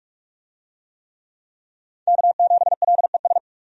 Morse code, heard as a keyed tone, about 46 wpm, 700 Hz. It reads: K7LES
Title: USA Call Signs - 5 Characters - 45wpm